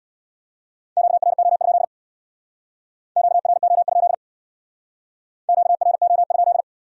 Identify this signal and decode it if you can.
Morse code, heard as a keyed tone, about 61 wpm, 700 Hz. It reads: QRO? QRO? QRO?
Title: Speed-Racing Q-Codes - 50wpm